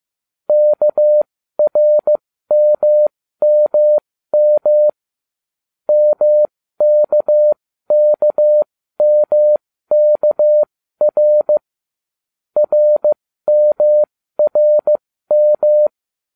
Morse code, heard as a keyed tone, about 15 wpm, spaced as for 13 wpm, 600 Hz. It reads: KRMMM MKKMKR RMRM